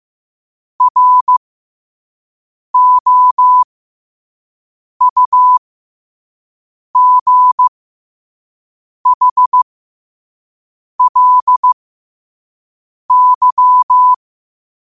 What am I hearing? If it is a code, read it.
Morse code, heard as a keyed tone, about 15 wpm, spaced as for 5 wpm, 1000 Hz. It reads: ROUGHLY
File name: qotdfiles / 5wpm